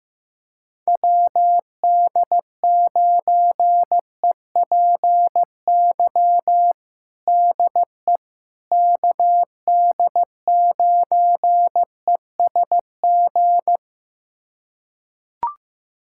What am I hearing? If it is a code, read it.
Morse code, heard as a keyed tone, about 15 wpm, 700 Hz. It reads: WD9EPY DE KD9ESG
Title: USA Callsign DE Callsign 15wpm